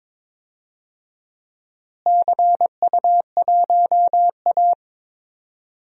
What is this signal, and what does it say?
Morse code, heard as a keyed tone, about 22 wpm, 700 Hz. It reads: CU1A
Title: DXCC-100 Call Signs - 4 Characters - Repeated 2x - 22wpm